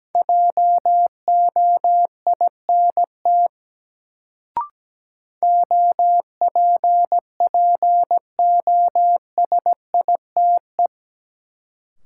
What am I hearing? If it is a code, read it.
Morse code, heard as a keyed tone, about 17 wpm, 700 Hz. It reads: JOINT OPPOSITE
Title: Sentences from Top 1300 Words - New Word Drill - 17wpm